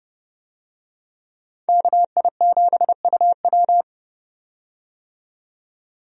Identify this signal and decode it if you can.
Morse code, heard as a keyed tone, about 30 wpm, 700 Hz. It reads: KI7UW
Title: USA Call Signs - 5 Characters - 30wpm